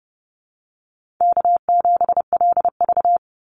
Morse code, heard as a keyed tone, about 30 wpm, 700 Hz. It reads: K7LV
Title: Call Signs Repeated 2x - 30wpm